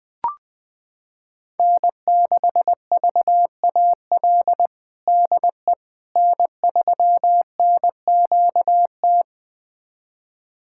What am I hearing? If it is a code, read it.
Morse code, heard as a keyed tone, about 20 wpm, 700 Hz. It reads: N6VAL DE N3NQT
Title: USA Callsign DE Callsign - 5 Characters - 20wpm